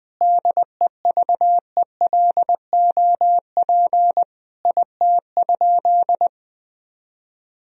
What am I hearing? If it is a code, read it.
Morse code, heard as a keyed tone, about 20 wpm, 700 Hz. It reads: DEVELOP IT?
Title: Sentences from Top 1000 Words - Review Part 2 - 20wpm